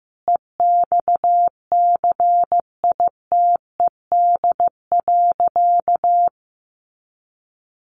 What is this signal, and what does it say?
Morse code, heard as a keyed tone, about 15 wpm, 700 Hz. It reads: EXCITED.
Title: Additional Easy Phrases 15wpm